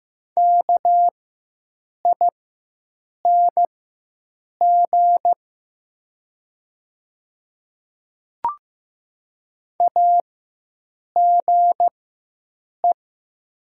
Morse code, heard as a keyed tone, about 15 wpm, spaced as for 7 wpm, 700 Hz. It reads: KING AGE
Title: 4x Character Spacing - Top 500 Words - 15wpm